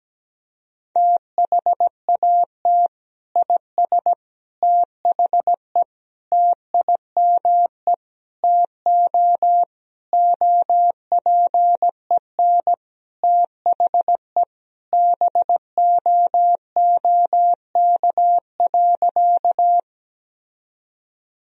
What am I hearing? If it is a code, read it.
Morse code, heard as a keyed tone, about 17 wpm, 700 Hz. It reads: THAT IS THE TIME TO OPEN THE BOOK.
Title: Sentences from Top 1000 Words - Review Part 1 - 17wpm